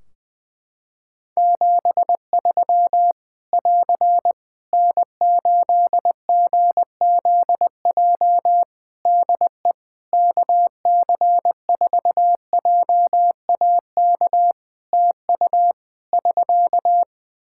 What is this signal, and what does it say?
Morse code, heard as a keyed tone, about 20 wpm, 700 Hz. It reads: 73 <AR> N8GZJ DE KC4JAK TU <SK>
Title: Practice QSO - Exchange 4 - Ending - 20wpm